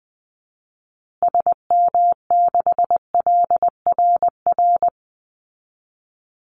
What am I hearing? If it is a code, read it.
Morse code, heard as a keyed tone, about 20 wpm, 700 Hz. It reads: SM6LRR
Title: Call Signs Repeated 2x - 20wpm